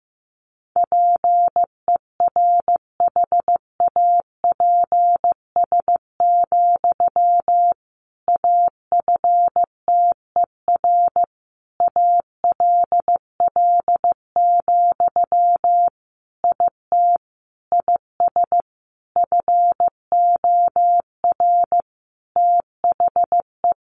Morse code, heard as a keyed tone, about 15 wpm, 700 Hz. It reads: PERHAPS, AFTER ALL, IT IS FOR THE
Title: Sentences from Top 500 Words 15wpm